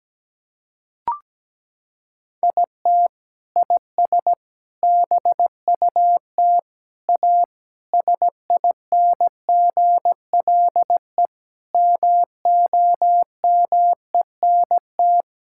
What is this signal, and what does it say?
Morse code, heard as a keyed tone, about 17 wpm, 700 Hz. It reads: IT IS BUT A SINGLE MOMENT
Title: Sentences from Top 1000 Words - Review Part 6 - 17wpm